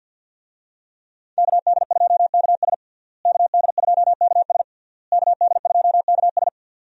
Morse code, heard as a keyed tone, about 50 wpm, 700 Hz. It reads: KD2KS KD2KS KD2KS
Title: USA Call Signs - 5 Characters - Repeated 3x - 50wpm